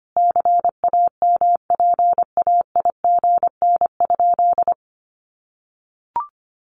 Morse code, heard as a keyed tone, about 25 wpm, 700 Hz. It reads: CAMPAIGN?
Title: Sentences from Top 1200 Words 25wpm